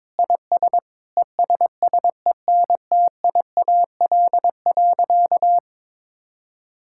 Morse code, heard as a keyed tone, about 22 wpm, 700 Hz. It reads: IS ESSENTIAL.
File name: Sentences with 7-Word Phrases - 22wpm